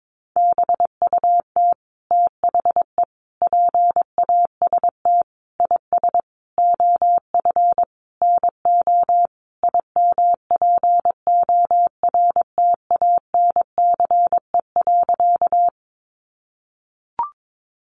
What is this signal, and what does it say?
Morse code, heard as a keyed tone, about 22 wpm, 700 Hz. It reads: BUT THE PAST IS OF NO IMPORTANCE.
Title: Sentences from Top 1200 Words 22wpm